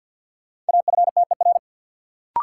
Morse code, heard as a keyed tone, about 53 wpm, 700 Hz. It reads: AFTER
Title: Sentences from Top 200 Words - New Word Drill - 50wpm